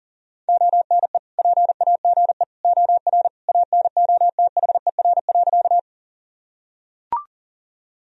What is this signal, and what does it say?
Morse code, heard as a keyed tone, about 37 wpm, 700 Hz. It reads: ONE PAGE OR ANOTHER.
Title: Sentences from Top 1000 Words - Review Part 1 - 40wpm